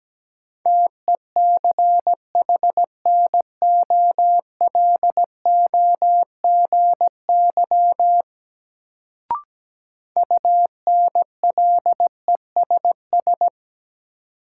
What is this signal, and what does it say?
Morse code, heard as a keyed tone, about 17 wpm, 700 Hz. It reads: TECHNOLOGY UNLESS
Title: Sentences from Top 1000 Words - Review Part 5 - New Word Drill - 17wpm